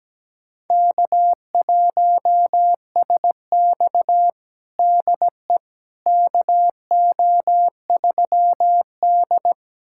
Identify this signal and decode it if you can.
Morse code, heard as a keyed tone, about 17 wpm, 700 Hz. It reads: K1SX DE KO3D
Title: USA Callsign DE Callsign - 4 Characters - 17wpm